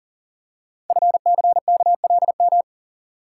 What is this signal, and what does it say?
Morse code, heard as a keyed tone, about 38 wpm, 690 Hz. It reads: FCKLM
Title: Code Groups - 5 Letters - 40wpm